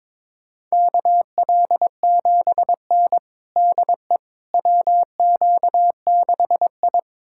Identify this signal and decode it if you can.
Morse code, heard as a keyed tone, about 22 wpm, 700 Hz. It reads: KL7N DE WQ6I
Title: USA Callsign DE Callsign - 4 Characters - 22wpm